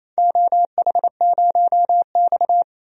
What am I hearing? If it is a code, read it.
Morse code, heard as a keyed tone, about 27 wpm, 700 Hz. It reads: OH0X
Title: DXCC-100 Call Signs - 4 Characters - 28wpm